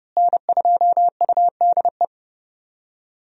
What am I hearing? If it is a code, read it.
Morse code, heard as a keyed tone, about 30 wpm, 700 Hz. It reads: N2UDE
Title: USA Call Signs - 5 Characters - Repeated 3x - 30wpm